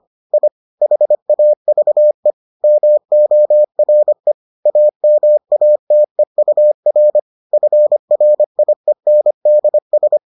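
Morse code, heard as a keyed tone, about 25 wpm, 600 Hz. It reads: I HAVE MORE AMATEUR FRIENDS
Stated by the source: Morse edition of the Foundations of Amateur Radio episodes.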